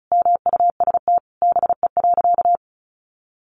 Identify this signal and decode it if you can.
Morse code, heard as a keyed tone, about 34 wpm, 700 Hz. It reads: MUST BE.
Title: Sentences from Top 1400 Words 35wpm